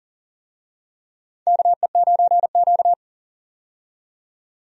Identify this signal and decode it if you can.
Morse code, heard as a keyed tone, about 40 wpm, 700 Hz. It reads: KE9Q
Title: USA Call Signs - 4 Characters - 40wpm